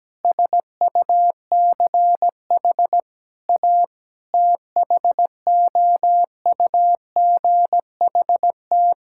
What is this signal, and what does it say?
Morse code, heard as a keyed tone, about 17 wpm, 700 Hz. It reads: SUCH A THOUGHT
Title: Sentences from Top 1000 Words - Review Part 6 - 17wpm